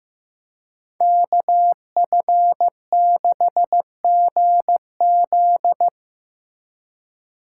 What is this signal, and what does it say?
Morse code, heard as a keyed tone, about 18 wpm, 700 Hz. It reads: KF6GZ
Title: Speed Racing - USA 2x2 Call Signs 15wpm